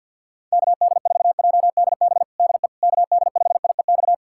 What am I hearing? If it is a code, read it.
Morse code, heard as a keyed tone, about 47 wpm, 700 Hz. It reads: KD4JDB DE KD5IEX
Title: USA Callsign DE Callsign 50wpm